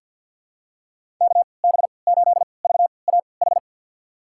Morse code, heard as a keyed tone, about 50 wpm, spaced as for 28 wpm, 700 Hz. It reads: KB8VAH
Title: USA Call Signs - 3x Character Spacing - 50wpm